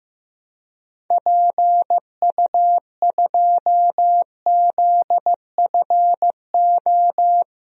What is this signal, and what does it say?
Morse code, heard as a keyed tone, about 15 wpm, 700 Hz. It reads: PU2ZFO
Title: DXCC-100 Call Signs - 6 Characters - Repeated 2x - 15wpm